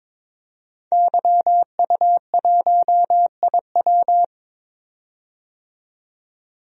Morse code, heard as a keyed tone, about 22 wpm, 700 Hz. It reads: YU1IW
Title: DXCC-100 Call Signs - 5 Characters - 22wpm